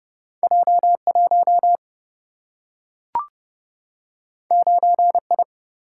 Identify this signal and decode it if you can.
Morse code, heard as a keyed tone, about 30 wpm, 700 Hz. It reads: J1 9I
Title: Single-Letter Single-Number 30wpm